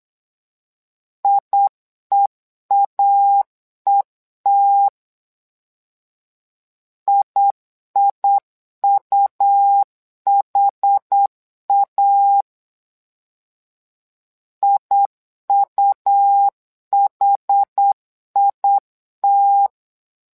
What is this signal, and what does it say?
Morse code, heard as a keyed tone, about 8 wpm, 800 Hz. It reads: IEAET IIUHA IUHIT
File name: LEZIONI / 2 UHDN / 2 UH 8wpm 2129